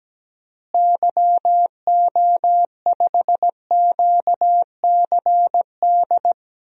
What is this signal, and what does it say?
Morse code, heard as a keyed tone, about 17 wpm, 700 Hz. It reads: YO5QCD
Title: DXCC-100 Call Signs - 6 Characters - Repeated 2x - 17wpm